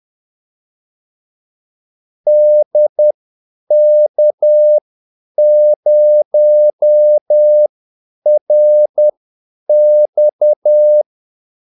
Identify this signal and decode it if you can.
Morse code, heard as a keyed tone, about 10 wpm, spaced as for 8 wpm, 600 Hz. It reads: DK0RX
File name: Callsigns-10-8wpm